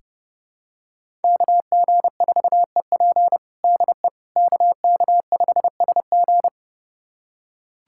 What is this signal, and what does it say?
Morse code, heard as a keyed tone, about 29 wpm, 700 Hz. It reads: KG4EP DE KK5SG
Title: USA Callsign DE Callsign - 5 Characters - 30wpm